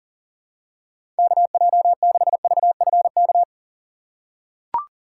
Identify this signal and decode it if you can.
Morse code, heard as a keyed tone, about 40 wpm, 700 Hz. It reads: KJ6VFK
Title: USA Call Signs - 6 Characters - 4x Character Spacing - 40wpm